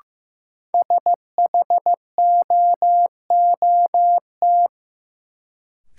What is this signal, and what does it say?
Morse code, heard as a keyed tone, about 15 wpm, 700 Hz. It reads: SHOOT
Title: Sentences from Top 1200 Words - New Word Drill - 15wpm